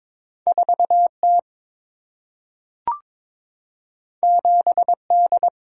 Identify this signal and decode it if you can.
Morse code, heard as a keyed tone, about 22 wpm, 700 Hz. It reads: 4T 7D
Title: Single-Letter Single-Number 22wpm